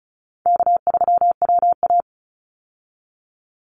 Morse code, heard as a keyed tone, about 34 wpm, 700 Hz. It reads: K3WA
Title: Easy Call Signs Repeated 2x - 35wpm